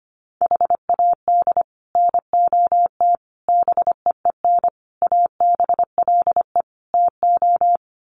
Morse code, heard as a keyed tone, about 25 wpm, 700 Hz. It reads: HAD NOT BEEN ABLE TO
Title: Sets of 5 Words Encore 25wpm